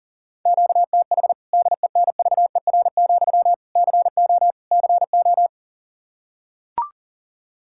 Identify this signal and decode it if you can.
Morse code, heard as a keyed tone, about 40 wpm, 700 Hz. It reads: QTH DENVER, CO CO
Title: K1USN SST QSO Practice 40wpm